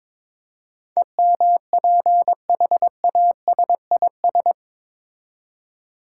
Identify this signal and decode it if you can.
Morse code, heard as a keyed tone, about 21 wpm, 700 Hz. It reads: EMPHASIS
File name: practice / 22wpm / Eight Letter Words - 22wpm